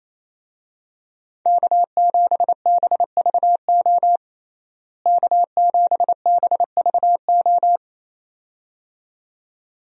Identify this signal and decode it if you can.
Morse code, heard as a keyed tone, about 28 wpm, 700 Hz. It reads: K7BVO K7BVO
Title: USA Call Signs - 5 Characters - Repeated 2x - 28wpm